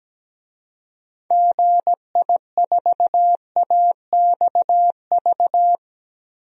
Morse code, heard as a keyed tone, about 17 wpm, 700 Hz. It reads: GI4AXV
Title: DXCC-100 Call Signs - 6 Characters - Repeated 3x - 17wpm